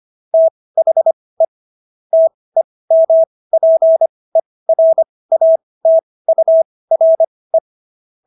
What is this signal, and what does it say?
Morse code, heard as a keyed tone, about 25 wpm, spaced as for 18 wpm, 650 Hz. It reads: THE TEMPERATURE